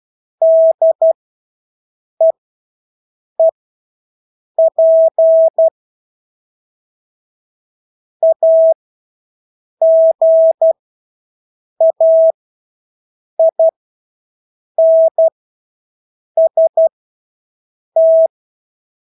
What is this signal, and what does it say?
Morse code, heard as a keyed tone, about 12 wpm, spaced as for 6 wpm, 650 Hz. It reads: DEEP AGAINST
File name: Engelske ord 12-6 wpm